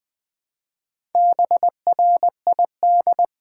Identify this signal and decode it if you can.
Morse code, heard as a keyed tone, about 20 wpm, 700 Hz. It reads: BRID
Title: Build A Word - 1 Letter At A Time - 5-Letter Words - 20wpm